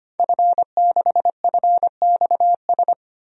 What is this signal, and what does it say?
Morse code, heard as a keyed tone, about 25 wpm, 700 Hz. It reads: F6FXS
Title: DXCC-100 Call Signs - 5 Characters - 25wpm